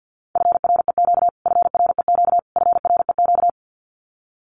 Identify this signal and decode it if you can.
Morse code, heard as a keyed tone, about 50 wpm, 700 Hz. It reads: FREQ FREQ FREQ
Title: Abbreviations Repeated 3x - 50wpm